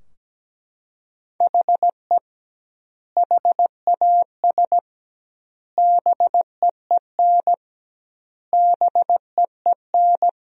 Morse code, heard as a keyed tone, about 17 wpm, 700 Hz. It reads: HE HAS BEEN BEEN
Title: 2x Word-Spacing - Sentences from Top 300 Words - 17wpm